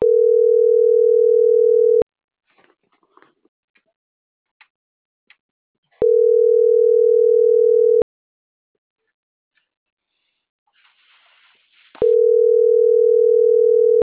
Ringback tone (North America), 440 + 480 Hz.